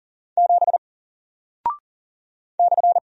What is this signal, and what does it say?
Morse code, heard as a keyed tone, about 39 wpm, 700 Hz. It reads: 7 /
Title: Single Character (Letter:Number:Punctuation:Prosign) 40wpm